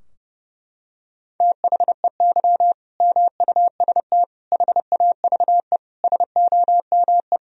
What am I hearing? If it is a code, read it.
Morse code, heard as a keyed tone, about 30 wpm, 700 Hz. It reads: THEY MUST HAVE SOME
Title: Sentences from Top 1000 Words - Review Part 3 - 30wpm